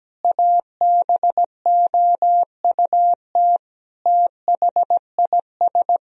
Morse code, heard as a keyed tone, about 17 wpm, 700 Hz. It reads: ABOUT THIS
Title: Sentences from Top 1000 Words - Review Part 2 - 17wpm